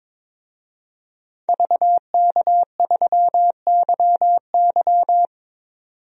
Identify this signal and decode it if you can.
Morse code, heard as a keyed tone, about 22 wpm, 700 Hz. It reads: VK3YY
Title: DXCC-100 Call Signs - 5 Characters - 22wpm